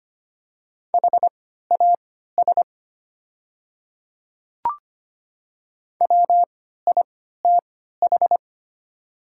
Morse code, heard as a keyed tone, about 25 wpm, spaced as for 14 wpm, 700 Hz. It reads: HAS WITH
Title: 3x Character Spacing - Top 100 Words - 25wpm